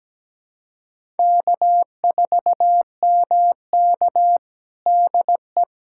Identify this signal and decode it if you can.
Morse code, heard as a keyed tone, about 17 wpm, 700 Hz. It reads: K4MK DE
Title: USA Callsign DE Callsign - 4 Characters - 17wpm